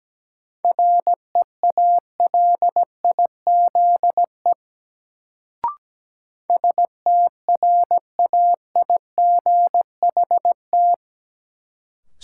Morse code, heard as a keyed tone, about 17 wpm, 700 Hz. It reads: REALIZE STRAIGHT
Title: Sentences from Top 1000 Words - New Word Drill - 17wpm